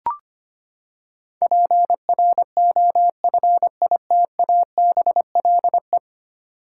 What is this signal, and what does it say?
Morse code, heard as a keyed tone, about 25 wpm, 700 Hz. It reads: PROFITABLE
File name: Ten Letter Words - 25wpm